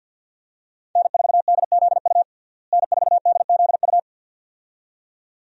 Morse code, heard as a keyed tone, about 48 wpm, 700 Hz. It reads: N4DZU N4DZU
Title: USA Call Signs - 5 Characters - Repeated 2x - 50wpm